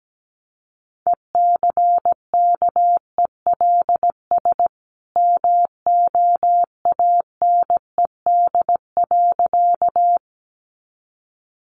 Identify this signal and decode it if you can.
Morse code, heard as a keyed tone, about 17 wpm, 700 Hz. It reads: ECKELS MOANED.
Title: A Sound Of Thunder 17wpm